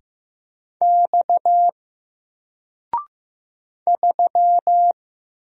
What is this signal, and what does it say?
Morse code, heard as a keyed tone, about 15 wpm, 700 Hz. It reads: X 3
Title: Single Letter-Number Alt 15wpm